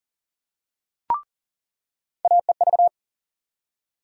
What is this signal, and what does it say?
Morse code, heard as a keyed tone, about 40 wpm, 700 Hz. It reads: AEV
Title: USA Call Sign Postfixes - 3 Letters 40wpm